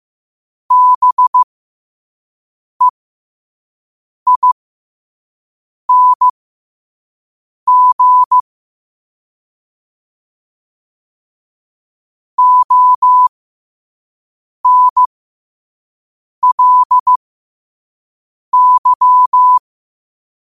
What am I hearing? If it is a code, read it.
Morse code, heard as a keyed tone, about 15 wpm, spaced as for 5 wpm, 1000 Hz. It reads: BEING ONLY
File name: qotdfiles / 5wpm